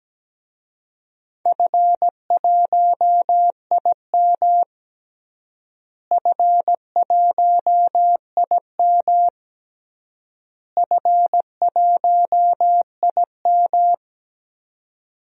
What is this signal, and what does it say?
Morse code, heard as a keyed tone, about 17 wpm, 700 Hz. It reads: F1IM F1IM F1IM
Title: DXCC-100 Call Signs - 4 Characters - Repeated 3x - 17wpm